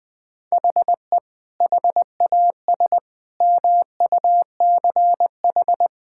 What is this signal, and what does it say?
Morse code, heard as a keyed tone, about 20 wpm, 700 Hz. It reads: HE HAS MUCH
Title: Sentences from Top 1000 Words - Review Part 5 - 20wpm